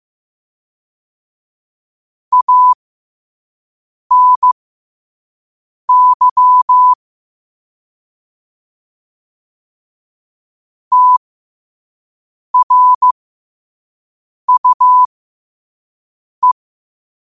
Morse code, heard as a keyed tone, about 15 wpm, spaced as for 5 wpm, 1000 Hz. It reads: ANY TRUE
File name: qotdfiles / 5wpm